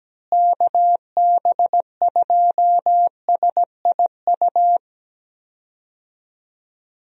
Morse code, heard as a keyed tone, about 17 wpm, 700 Hz. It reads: KB2SIU
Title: USA Call Signs - 6 Characters - Repeated 2x - 17wpm